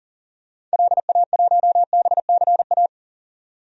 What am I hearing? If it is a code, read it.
Morse code, heard as a keyed tone, about 39 wpm, 700 Hz. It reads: LA1BCA
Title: DXCC-100 Call Signs - 6 Characters - 40wpm